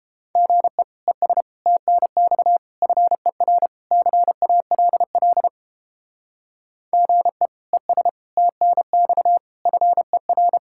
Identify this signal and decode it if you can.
Morse code, heard as a keyed tone, about 31 wpm, 700 Hz. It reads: GE ES TNX FER CALL GE ES TNX FER